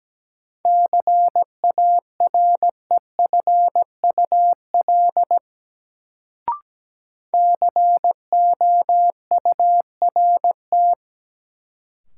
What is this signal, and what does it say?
Morse code, heard as a keyed tone, about 17 wpm, 700 Hz. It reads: CAREFUL COURT
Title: Sentences from Top 1600 Words - New Word Drill Alt - 17wpm